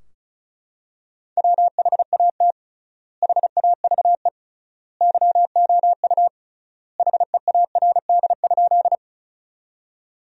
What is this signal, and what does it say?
Morse code, heard as a keyed tone, about 36 wpm, 700 Hz. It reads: WHAT HAVE YOU HEARD?
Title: 3x Word-Spacing - Sentences from Top 400 Words - 35wpm